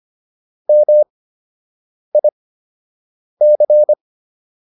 Morse code, heard as a keyed tone, about 25 wpm, spaced as for 7 wpm, 600 Hz. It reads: MIC